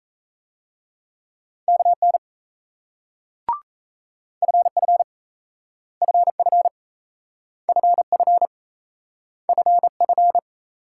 Morse code, heard as a keyed tone, about 35 wpm, 700 Hz. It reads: KN FF FF FF FF